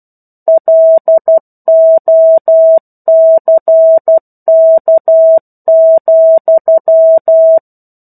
Morse code, heard as a keyed tone, about 12 wpm, 650 Hz. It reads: LOCK,
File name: Chapter-01 12 WPM